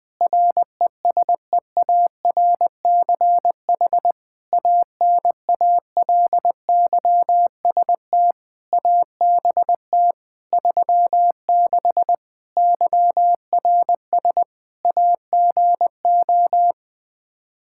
Morse code, heard as a keyed tone, about 20 wpm, 700 Hz. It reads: RESEARCH ANALYST ABT 36 YRS AGO
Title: Practice QSO - Exchange 3 - More Info - 20wpm